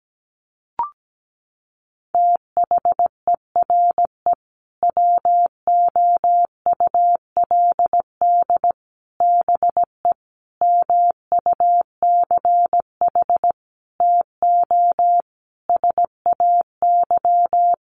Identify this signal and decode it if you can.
Morse code, heard as a keyed tone, about 17 wpm, 700 Hz. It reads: THERE WOULD BE MUCH TO SAY
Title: Sentences from Top 1200 Words 17wpm